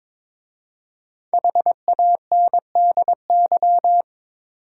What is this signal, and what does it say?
Morse code, heard as a keyed tone, about 22 wpm, 700 Hz. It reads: HANDY
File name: 5 Letter Words - First 3 Letters Given - 22wpm